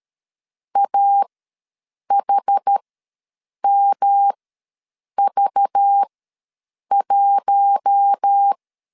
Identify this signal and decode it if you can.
Morse code, heard as a keyed tone, about 13 wpm, spaced as for 7 wpm, 800 Hz. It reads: AHMV1
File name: LEZIONI / 5 LBVC / 5 LBV ws 12wpm 3015